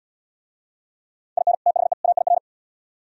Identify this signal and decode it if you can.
Morse code, heard as a keyed tone, about 72 wpm, 700 Hz. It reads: IN THE AIR